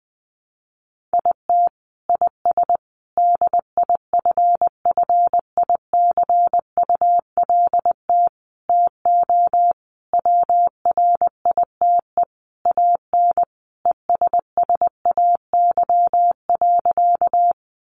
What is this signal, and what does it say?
Morse code, heard as a keyed tone, about 20 wpm, 700 Hz. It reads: IT IS DIFFICULT TO WRITE AN ESSAY.
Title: Sentences from Top 1500 Words 20wpm